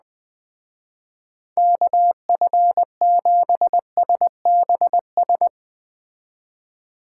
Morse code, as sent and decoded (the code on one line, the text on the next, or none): -.- ..-. --... ... -... ...
KF7SBS